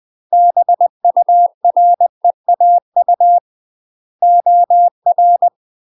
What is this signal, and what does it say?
Morse code, heard as a keyed tone, about 20 wpm, 700 Hz. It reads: BUREAU OR
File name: UK MASTER QSOs TRAINING / 20WPM / UK MASTER QSOs 0030